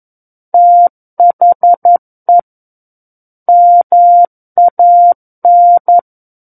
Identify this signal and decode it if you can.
Morse code, heard as a keyed tone, about 11 wpm, 700 Hz. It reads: THE MAN
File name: Book One / 02-Chapter 2 - 11 WPM